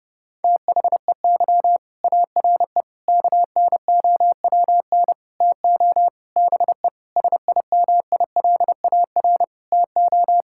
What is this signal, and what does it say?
Morse code, heard as a keyed tone, about 32 wpm, 700 Hz. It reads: THEY ARE KNOWN TO BE SIMILAR TO
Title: Sentences from Top 1000 Words - Review Part 5 - 30wpm